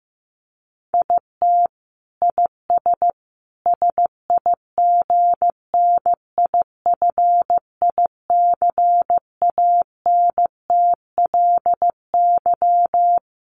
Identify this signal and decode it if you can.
Morse code, heard as a keyed tone, about 15 wpm, 700 Hz. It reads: IT IS SIGNIFICANTLY
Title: Sentences from Top 1500 Words 15wpm